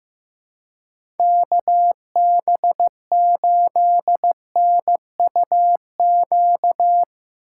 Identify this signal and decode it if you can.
Morse code, heard as a keyed tone, about 15 wpm, 700 Hz. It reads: KB8NUQ
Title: USA Call Signs - 6 Characters - 2x Character Spacing - 15wpm